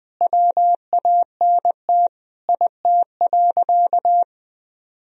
Morse code, heard as a keyed tone, about 19 wpm, 700 Hz. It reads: WANT IT.
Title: Sentences from Top 300 Words and Abbreviations 20wpm